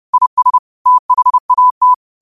Morse code, heard as a keyed tone, about 30 wpm, 1000 Hz. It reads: IS THAT
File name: qotdfiles / 30wpm